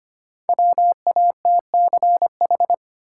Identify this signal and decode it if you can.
Morse code, heard as a keyed tone, about 25 wpm, 700 Hz. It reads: WATCH
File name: 5 Letter Words - First 3 Letters Given - 25wpm